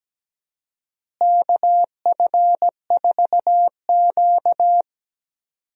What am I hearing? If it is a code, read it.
Morse code, heard as a keyed tone, about 17 wpm, 700 Hz. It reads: KF4Q
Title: USA Call Signs - 4 Characters - Repeated 2x - 17wpm